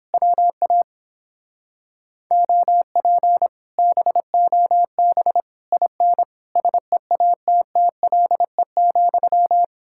Morse code, heard as a keyed tone, about 31 wpm, 700 Hz. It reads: WA OP BOB IN SEATTLE,